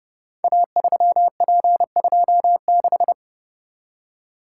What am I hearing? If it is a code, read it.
Morse code, heard as a keyed tone, about 30 wpm, 700 Hz. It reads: A3P26
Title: Code Groups - 5 Letter:Numbers - 30wpm